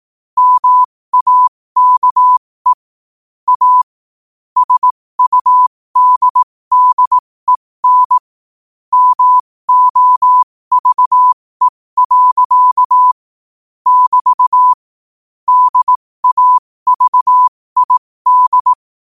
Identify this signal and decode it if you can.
Morse code, heard as a keyed tone, about 18 wpm, spaced as for 15 wpm, 1000 Hz. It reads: MAKE A SUDDEN MOVE. <BT> DAVID
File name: qotdfiles / 15wpm